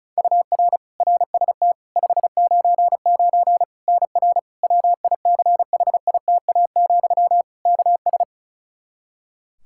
Morse code, heard as a keyed tone, about 36 wpm, 700 Hz. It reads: UR RST 599 NR WICHITA, KS
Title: K1USN SST QSO Practice 35wpm